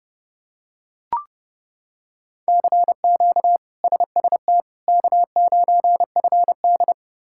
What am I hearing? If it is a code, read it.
Morse code, heard as a keyed tone, about 32 wpm, 700 Hz. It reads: CQ SST K9FD
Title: K1USN SST QSO Practice - 30wpm